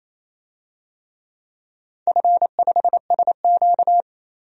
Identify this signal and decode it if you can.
Morse code, heard as a keyed tone, about 28 wpm, 690 Hz. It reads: F5SQ
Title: DXCC-100 Call Signs - 4 Characters - 28wpm